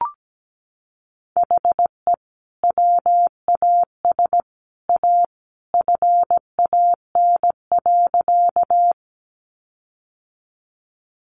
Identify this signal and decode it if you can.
Morse code, heard as a keyed tone, about 17 wpm, 700 Hz. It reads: HE WAS A FAN.
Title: Sentences from Top 1300 Words 17wpm